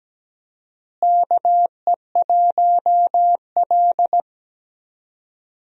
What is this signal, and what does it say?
Morse code, heard as a keyed tone, about 17 wpm, 700 Hz. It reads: KE1L
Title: USA Call Signs - 4 Characters - Repeated 2x - 17wpm